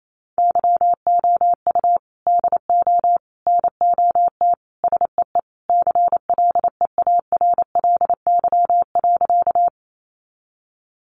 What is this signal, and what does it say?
Morse code, heard as a keyed tone, about 28 wpm, 700 Hz. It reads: YOU DO NOT SEE CLEARLY.
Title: Sentences from Top 1300 Words 28wpm